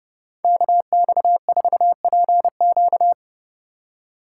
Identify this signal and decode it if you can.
Morse code, heard as a keyed tone, about 30 wpm, 700 Hz. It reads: KX4PQ
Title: K1USN SST QSO Practice - 30wpm